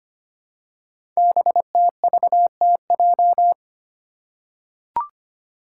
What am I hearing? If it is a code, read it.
Morse code, heard as a keyed tone, about 25 wpm, 700 Hz. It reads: BTVTJ
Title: Code Groups - 5 Letters - 25wpm